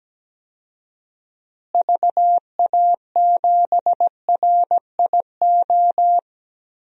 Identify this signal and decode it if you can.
Morse code, heard as a keyed tone, about 17 wpm, 700 Hz. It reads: VA7RIO
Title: DXCC-100 Call Signs - 6 Characters - Repeated 2x - 17wpm